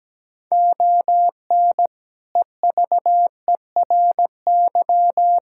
Morse code, heard as a keyed tone, about 17 wpm, 700 Hz. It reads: ON EVERY
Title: Sentences from Top 1000 Words - Review Part 6 - 17wpm